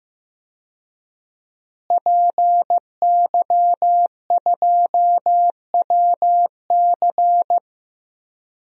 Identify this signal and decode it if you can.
Morse code, heard as a keyed tone, about 15 wpm, 700 Hz. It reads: PY2WC
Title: DXCC-100 Call Signs - 5 Characters - Repeated 2x - 15wpm